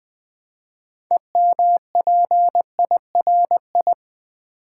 Morse code, heard as a keyed tone, about 20 wpm, 700 Hz. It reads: EMPIRI
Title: Build A Word - 3 Letters At A Time - 9-Letter Words - 20wpm